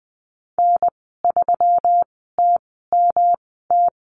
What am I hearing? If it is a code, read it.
Morse code, heard as a keyed tone, about 20 wpm, spaced as for 14 wpm, 700 Hz. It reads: N3TMT
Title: 186 Lesson 30g - Letter b - Call Signs 20wpm